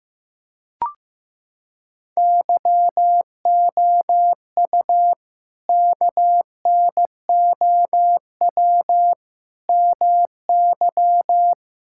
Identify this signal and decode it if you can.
Morse code, heard as a keyed tone, about 15 wpm, 700 Hz. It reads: YOU KNOW MY
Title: Sentences from Top 1600 Words 15wpm